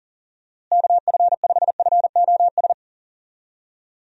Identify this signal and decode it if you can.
Morse code, heard as a keyed tone, about 40 wpm, 700 Hz. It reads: KF5FOS
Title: USA Call Signs - 6 Characters - Repeated 3x - 40wpm